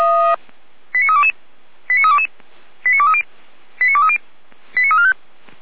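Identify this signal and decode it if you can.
Five-tone selective call (ZVEI), heard as quick runs of five tones, tones of about 70 ms. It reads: 89322 89322 89230 89230 89345
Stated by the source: Five-tone ZVEI selective call received on an SDR.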